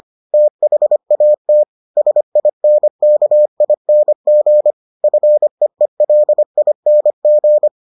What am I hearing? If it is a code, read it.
Morse code, heard as a keyed tone, about 25 wpm, 600 Hz. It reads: THAT SINKING FEELING